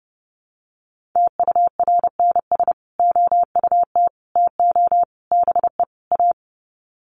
Morse code, heard as a keyed tone, about 28 wpm, 700 Hz. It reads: TURNS OUT TO BE A
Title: Sets of 5 Words Encore 30wpm